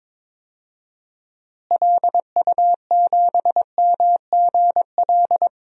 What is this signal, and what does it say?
Morse code, heard as a keyed tone, about 22 wpm, 700 Hz. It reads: LU7MGL
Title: DXCC-100 Call Signs - 6 Characters - Repeated 2x - 22wpm